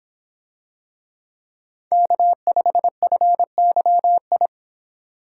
Morse code, heard as a keyed tone, about 24 wpm, 700 Hz. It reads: K5FYI